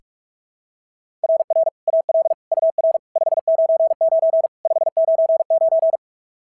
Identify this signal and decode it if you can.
Morse code, heard as a keyed tone, about 44 wpm, 690 Hz. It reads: RR AL UR 599 599
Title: SOTA QSO Practice 45wpm